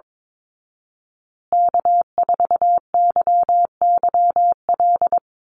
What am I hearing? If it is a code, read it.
Morse code, heard as a keyed tone, about 22 wpm, 700 Hz. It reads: K4YYL
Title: Call Signs Repeated 2x - 22wpm